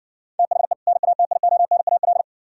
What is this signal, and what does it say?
Morse code, heard as a keyed tone, about 60 wpm, 700 Hz. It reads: THE NATIONAL